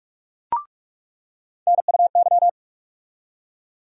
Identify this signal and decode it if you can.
Morse code, heard as a keyed tone, about 45 wpm, 700 Hz. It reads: NUY
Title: USA Call Sign Postfixes - 3 Letters 45wpm